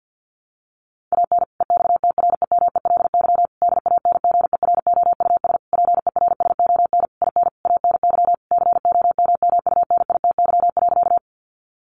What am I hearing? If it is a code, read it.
Morse code, heard as a keyed tone, about 49 wpm, 700 Hz. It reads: AN EXTREMELY DANGEROUS PERSON IN ANY COMMUNITY.
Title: Sentences from Top 1200 Words 50wpm